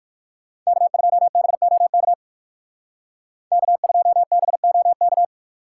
Morse code, heard as a keyed tone, about 64 wpm, spaced as for 56 wpm, 700 Hz. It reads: K2BOK K2BOK
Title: Speed Racing - USA 1x3 Call Signs 45wpm